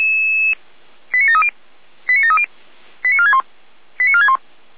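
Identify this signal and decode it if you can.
Five-tone selective call (ZVEI), heard as quick runs of five tones, tones of about 70 ms. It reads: 89730 89730 89461 89461
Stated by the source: ZVEI five-tone selective call received by SDR.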